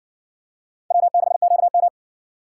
Morse code, heard as a keyed tone, about 56 wpm, 700 Hz. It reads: W6QM